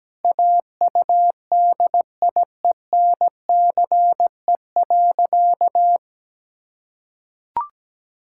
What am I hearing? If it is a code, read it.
Morse code, heard as a keyed tone, about 17 wpm, 700 Hz. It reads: AUDIENCE.
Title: Sentences from Top 1000 Words - Review Part 5 - 17wpm